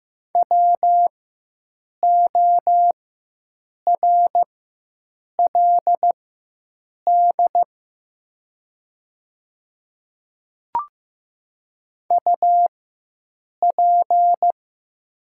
Morse code, heard as a keyed tone, about 15 wpm, spaced as for 7 wpm, 700 Hz. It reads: WORLD UP
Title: 4x Character Spacing - Top 200 Words - 15wpm